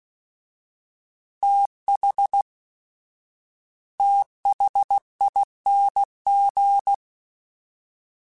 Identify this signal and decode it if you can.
Morse code, heard as a keyed tone, about 16 wpm, 800 Hz. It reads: TH THING
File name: TH / TH mp3s / TH Prefixes 1x1 @15